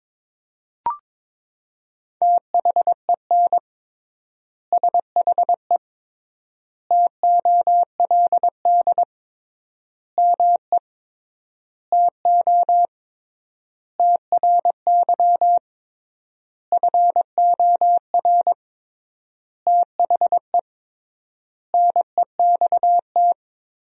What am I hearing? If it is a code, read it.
Morse code, heard as a keyed tone, about 22 wpm, 700 Hz. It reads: THEN SHE TOLD ME TO TRY FOR THE NEXT
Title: 3x Word-Spacing - Sentences from Top 400 Words - 22wpm